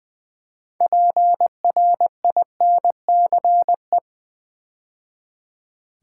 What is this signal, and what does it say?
Morse code, heard as a keyed tone, about 20 wpm, 700 Hz. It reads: PRINCE
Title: Build A Word - 2 Letters At A Time - 8-Letter Words - 20wpm